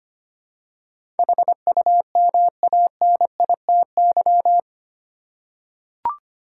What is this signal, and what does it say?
Morse code, heard as a keyed tone, about 25 wpm, 700 Hz. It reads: HUMANITY
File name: Eight Letter Words - 25wpm